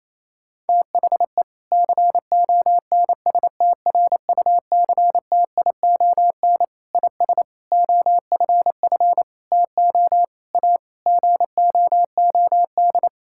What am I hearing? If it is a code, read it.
Morse code, heard as a keyed tone, about 28 wpm, 700 Hz. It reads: THE CONSTRUCTION IS OFF TO A GOOD
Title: Sentences from Top 1600 Words 28wpm